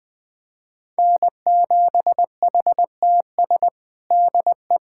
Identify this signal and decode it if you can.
Morse code, heard as a keyed tone, about 19 wpm, 700 Hz. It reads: N7HTS DE
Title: USA Callsign DE Callsign 20wpm